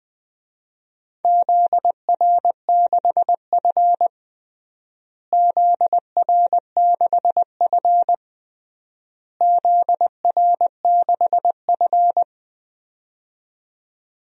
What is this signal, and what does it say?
Morse code, heard as a keyed tone, about 21 wpm, 700 Hz. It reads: ZR6F ZR6F ZR6F
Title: DXCC-100 Call Signs - 4 Characters - Repeated 3x - 20wpm